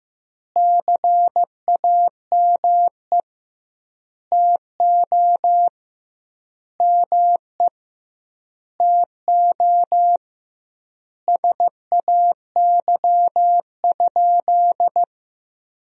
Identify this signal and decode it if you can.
Morse code, heard as a keyed tone, about 15 wpm, 700 Hz. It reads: CAME TO ME TO SAY?
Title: 2x Word-Spacing - Sentences from Top 200 Words - 15wpm